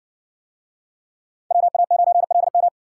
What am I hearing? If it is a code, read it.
Morse code, heard as a keyed tone, about 60 wpm, 700 Hz. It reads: WA9LM